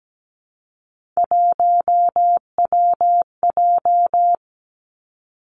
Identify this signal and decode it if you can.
Morse code, heard as a keyed tone, about 17 wpm, 700 Hz. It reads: 1WJ
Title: Three Challenging Letter-Number Combinations 17wpm